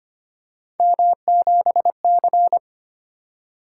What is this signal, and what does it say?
Morse code, heard as a keyed tone, about 25 wpm, 700 Hz. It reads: M7C
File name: Three Letter-Number Combinations - 25wpm